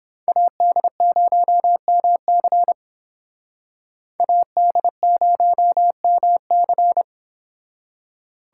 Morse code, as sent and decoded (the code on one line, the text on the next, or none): .- -.. ----- -- -.-. / .- -.. ----- -- -.-.
AD0MC AD0MC